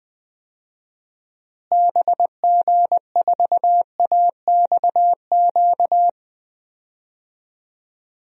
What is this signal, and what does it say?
Morse code, heard as a keyed tone, about 20 wpm, 700 Hz. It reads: BG4AXQ
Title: DXCC-100 Call Signs - 6 Characters - Repeated 2x - 20wpm